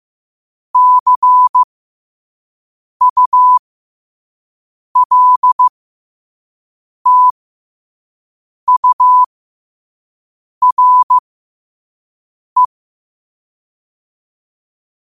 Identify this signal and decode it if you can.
Morse code, heard as a keyed tone, about 15 wpm, spaced as for 5 wpm, 1000 Hz. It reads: CULTURE
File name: qotdfiles / 5wpm